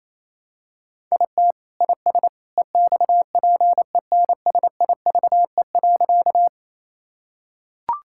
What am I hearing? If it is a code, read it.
Morse code, heard as a keyed tone, about 27 wpm, 700 Hz. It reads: IT IS EXPENSIVE.
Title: Sentences from Top 1000 Words - Review Part 6 - 28wpm